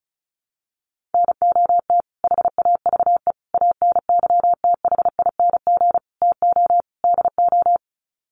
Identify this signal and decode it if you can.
Morse code, heard as a keyed tone, about 34 wpm, 700 Hz. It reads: NOT HAVE ANYTHING TO DO
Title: Sets of 5 Words Encore 35wpm